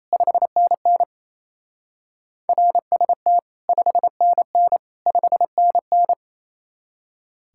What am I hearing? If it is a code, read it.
Morse code, heard as a keyed tone, about 36 wpm, 700 Hz. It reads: 5NN RST 5NN 5NN